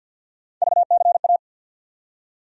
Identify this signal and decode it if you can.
Morse code, heard as a keyed tone, about 47 wpm, 700 Hz. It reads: VCA
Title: USA Call Sign Postfixes - 3 Letters 50wpm